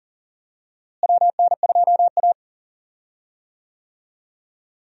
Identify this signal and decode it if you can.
Morse code, heard as a keyed tone, about 40 wpm, 700 Hz. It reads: WN2A
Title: USA Call Signs - 4 Characters - Repeated 2x - 40wpm